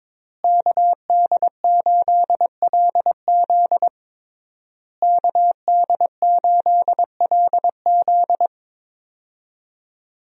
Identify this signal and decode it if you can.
Morse code, heard as a keyed tone, about 22 wpm, 700 Hz. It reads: KD8LZ KD8LZ
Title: USA Call Signs - 5 Characters - Repeated 3x - 22wpm